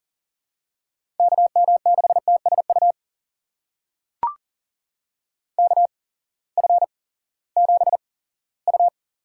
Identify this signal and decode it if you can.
Morse code, heard as a keyed tone, about 40 wpm, spaced as for 20 wpm, 700 Hz. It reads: KM6TSU KF7U
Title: USA Call Signs - 6 Characters - 8x Character Spacing - 40wpm